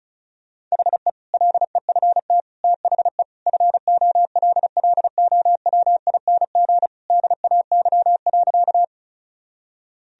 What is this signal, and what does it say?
Morse code, heard as a keyed tone, about 34 wpm, 700 Hz. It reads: HE LEFT THE FOLLOWING DAY.
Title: Sentences from Top 1000 Words - Review Part 4 - 35wpm